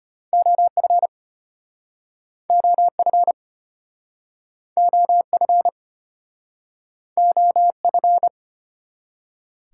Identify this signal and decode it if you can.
Morse code, heard as a keyed tone, about 34 wpm, 700 Hz. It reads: OF OF OF OF